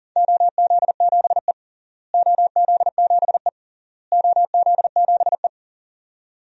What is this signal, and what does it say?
Morse code, heard as a keyed tone, about 39 wpm, 700 Hz. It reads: OZ7E OZ7E OZ7E
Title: DXCC-100 Call Signs - 4 Characters - Repeated 3x - 40wpm